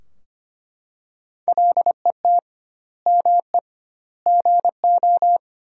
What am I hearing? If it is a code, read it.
Morse code, heard as a keyed tone, about 25 wpm, 700 Hz. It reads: LET ME GO
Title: 2x Word-Spacing - Sentences from Top 300 Words - 25wpm